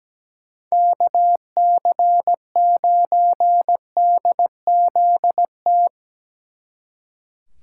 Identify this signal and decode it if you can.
Morse code, heard as a keyed tone, about 20 wpm, 700 Hz. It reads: KC9DZT
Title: Speed Racing - USA 2x3 Call Signs 17wpm